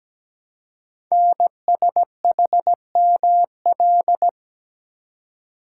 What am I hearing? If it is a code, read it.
Morse code, heard as a keyed tone, about 17 wpm, 700 Hz. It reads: NSHML
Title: Code Groups - 5 Letters - 17wpm